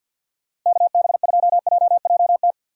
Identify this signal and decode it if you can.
Morse code, heard as a keyed tone, about 52 wpm, 700 Hz. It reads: KB2JJT
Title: USA Call Signs - 6 Characters - 4x Character Spacing - 50wpm